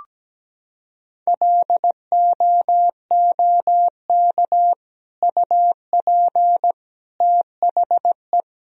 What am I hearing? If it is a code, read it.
Morse code, heard as a keyed tone, about 17 wpm, 700 Hz. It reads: LOOK UP THE
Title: Sentences from Top 1000 Words - Review Part 4 - 17wpm